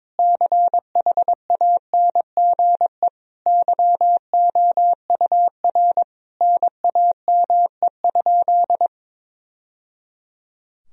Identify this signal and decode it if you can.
Morse code, heard as a keyed tone, about 22 wpm, 700 Hz. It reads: CHANGE YOUR NAME?
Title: Sentences from Top 1000 Words - Review Part 1 - 22wpm